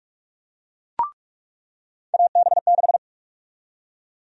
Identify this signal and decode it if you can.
Morse code, heard as a keyed tone, about 46 wpm, 700 Hz. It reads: AB6
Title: USA Call Sign Prefixes - 45wpm